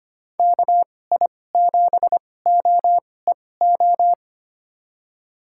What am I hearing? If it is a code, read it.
Morse code, heard as a keyed tone, about 25 wpm, spaced as for 18 wpm, 700 Hz. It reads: KI7OEO
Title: USA Call Signs - 6 Characters - 2x Character Spacing - 25wpm